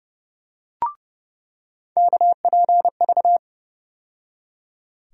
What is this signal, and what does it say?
Morse code, heard as a keyed tone, about 30 wpm, 700 Hz. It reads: KPV
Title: USA Call Sign Postfixes - 3 Letters 30wpm